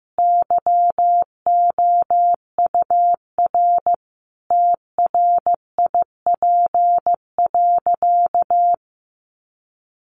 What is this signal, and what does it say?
Morse code, heard as a keyed tone, about 14 wpm, 700 Hz. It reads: YOUR TRIP.
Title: Sentences from Top 1500 Words 15wpm